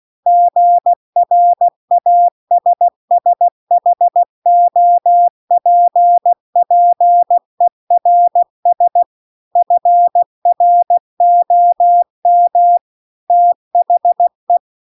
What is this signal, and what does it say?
Morse code, heard as a keyed tone, about 16 wpm, 700 Hz. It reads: GRASSHOPPERS FROM THE